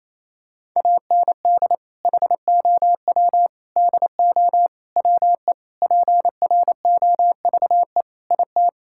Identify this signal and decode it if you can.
Morse code, heard as a keyed tone, about 28 wpm, 700 Hz. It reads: AND HOW DO WE PROVE IT
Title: Sentences from Top 1600 Words 28wpm